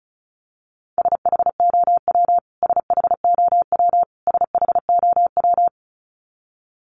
Morse code, heard as a keyed tone, about 34 wpm, 700 Hz. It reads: SHOW SHOW SHOW
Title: Top 200 Words Repeated 35wpm